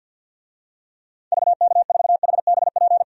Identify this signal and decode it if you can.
Morse code, heard as a keyed tone, about 50 wpm, 700 Hz. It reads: VK4HBP
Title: DXCC-100 Call Signs - 6 Characters - 50wpm